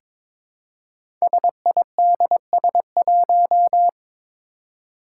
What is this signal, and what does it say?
Morse code, heard as a keyed tone, about 22 wpm, 700 Hz. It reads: SIDS1
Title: Five Challenging Letter-Number Combinations 22wpm